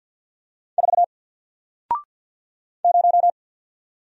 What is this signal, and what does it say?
Morse code, heard as a keyed tone, about 48 wpm, 700 Hz. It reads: 4 0
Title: Single Letter-Number Alt 50wpm